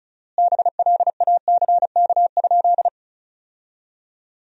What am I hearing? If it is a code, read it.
Morse code, heard as a keyed tone, about 36 wpm, 700 Hz. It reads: BLACK?
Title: 4x Word-Spacing - Sentences from Top 400 Words - 35wpm